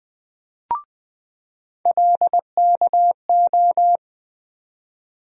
Morse code, heard as a keyed tone, about 20 wpm, 700 Hz. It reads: LKO
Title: USA Call Sign Postfixes - 3 Letters 20wpm